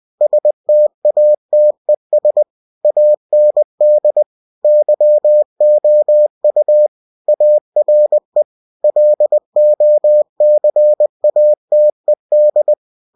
Morse code, heard as a keyed tone, about 20 wpm, 600 Hz. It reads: STATES AND YOU ARE LOCATED
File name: cw-books / winnie-the-pooh / winnie-20wpm-page-0114